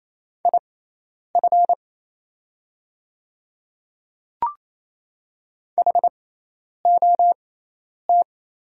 Morse code, heard as a keyed tone, about 27 wpm, spaced as for 10 wpm, 700 Hz. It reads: IF HOT
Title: 6x Character Spacing - Top 300 Words - 28wpm